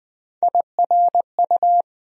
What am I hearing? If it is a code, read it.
Morse code, heard as a keyed tone, about 20 wpm, 700 Hz. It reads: IRU
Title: USA Call Sign Postfixes - 3 Letters 20wpm